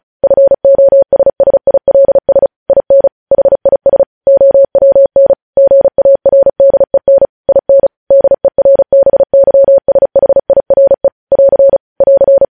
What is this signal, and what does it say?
Morse code, heard as a keyed tone, about 36 wpm, 550 Hz. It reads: FOSSILS IN HIS OWN GARDEN IN DERBYSHIRE <AR> <AR>
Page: News headlines in Morse code at 35 wpm